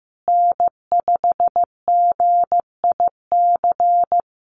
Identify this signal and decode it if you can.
Morse code, heard as a keyed tone, about 14 wpm, 700 Hz. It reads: N5GIC
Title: USA Call Signs Repeated 2x 15wpm